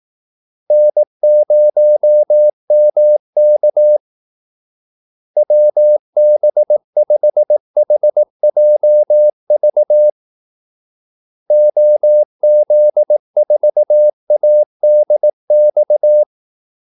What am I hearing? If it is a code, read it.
Morse code, heard as a keyed tone, about 18 wpm, 600 Hz. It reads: N0MK WB5HJV OZ4ADX
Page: Kaldesignaler 18 wpm